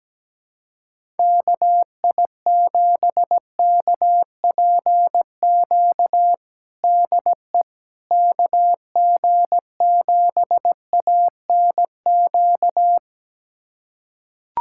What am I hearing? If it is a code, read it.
Morse code, heard as a keyed tone, about 17 wpm, 700 Hz. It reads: KI7KPQ DE KG7ANQ
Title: USA Callsign DE Callsign - 6 Characters - 17wpm